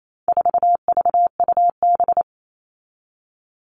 Morse code, heard as a keyed tone, about 28 wpm, 700 Hz. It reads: 4VUB
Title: All Cut Numbers - 4-Digit 28wpm